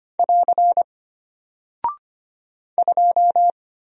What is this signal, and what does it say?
Morse code, heard as a keyed tone, about 25 wpm, 700 Hz. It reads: <AR> 2
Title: Single Character (Letter:Number:Punctuation:Prosign) Alt 25wpm